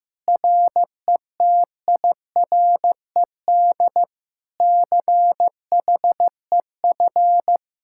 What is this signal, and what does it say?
Morse code, heard as a keyed tone, about 15 wpm, 700 Hz. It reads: RETIRED CHEF
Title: Speed-Racing Common QSO Phrases Encore - 15wpm